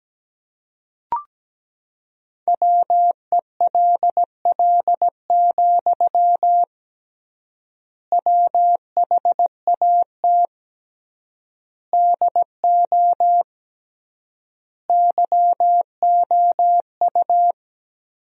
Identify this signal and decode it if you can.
Morse code, heard as a keyed tone, about 17 wpm, 700 Hz. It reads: WELL, WHAT DO YOU
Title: 3x Word-Spacing - Sentences from Top 200 Words - 17wpm